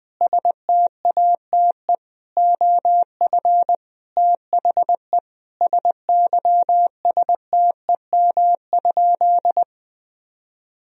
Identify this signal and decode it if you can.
Morse code, heard as a keyed tone, about 21 wpm, 700 Hz. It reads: STATE OF THE SYSTEM?
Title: Sentences from Top 1000 Words - Review Part 3 - 20wpm